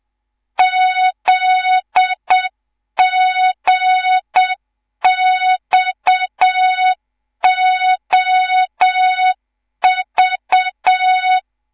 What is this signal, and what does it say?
Morse code, heard as a keyed tone, about 7 wpm, 3700 Hz. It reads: ZGXOV
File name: Morze-HA2MN / hang / 11 GYAKORLAT